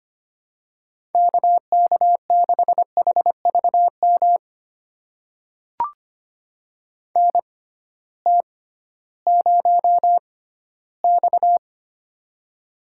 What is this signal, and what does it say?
Morse code, heard as a keyed tone, about 25 wpm, spaced as for 15 wpm, 700 Hz. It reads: KK6HVM NT0X
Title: USA Call Signs - 6x Character Spacing - 25wpm